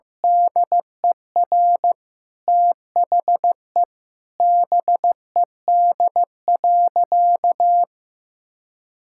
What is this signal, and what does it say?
Morse code, heard as a keyed tone, about 15 wpm, 700 Hz. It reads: DER THE BED.
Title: Sentences from Top 1000 Words - Review Part 6 - 15wpm